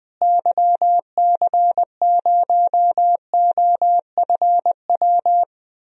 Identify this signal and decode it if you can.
Morse code, heard as a keyed tone, about 21 wpm, 700 Hz. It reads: YC0OFW
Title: DXCC-100 Call Signs - 6 Characters - Repeated 3x - 20wpm